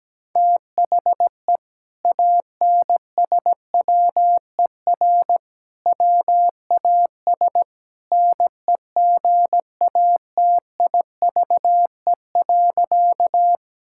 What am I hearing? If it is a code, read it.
Morse code, heard as a keyed tone, about 17 wpm, 700 Hz. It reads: THE ANSWER WAS NEGATIVE.
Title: Sentences from Top 1000 Words - Review Part 6 - 17wpm